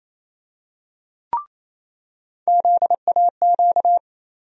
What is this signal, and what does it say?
Morse code, heard as a keyed tone, about 28 wpm, 700 Hz. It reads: ZAQ
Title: USA Call Sign Postfixes - 3 Letters 28wpm